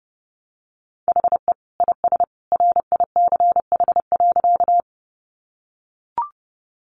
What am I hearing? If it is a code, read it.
Morse code, heard as a keyed tone, about 28 wpm, 700 Hz. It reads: HE IS RICH.
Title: Sentences from Top 1100 Words 30wpm